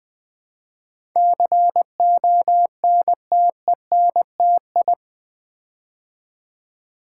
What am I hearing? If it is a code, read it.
Morse code, heard as a keyed tone, about 20 wpm, 700 Hz. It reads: CONTENTI
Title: Build A Word - 4 Letters At A Time - 10-Letter Words - 20wpm